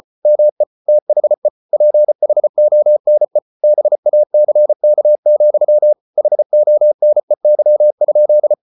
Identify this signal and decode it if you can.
Morse code, heard as a keyed tone, about 34 wpm, 600 Hz. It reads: ME THE PHONE BACK, HONEY?